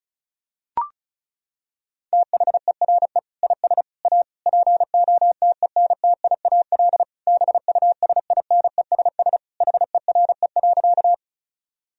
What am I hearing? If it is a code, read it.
Morse code, heard as a keyed tone, about 36 wpm, 700 Hz. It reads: THERE IS A POTENTIAL BUSINESS HERE.
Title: Sentences from Top 1000 Words - Review Part 4 - 35wpm